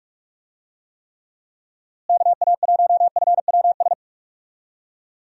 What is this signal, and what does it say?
Morse code, heard as a keyed tone, about 43 wpm, 700 Hz. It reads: KA1FWS
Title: USA Call Signs - 6 Characters - Repeated 2x - 45wpm